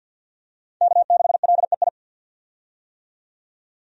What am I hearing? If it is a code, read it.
Morse code, heard as a keyed tone, about 53 wpm, 700 Hz. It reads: K6LEI
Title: USA Call Signs - 5 Characters - 50wpm